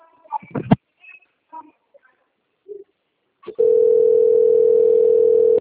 Ringback tone (North America), 440 + 480 Hz.
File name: rg-1600-unknown- - - .23031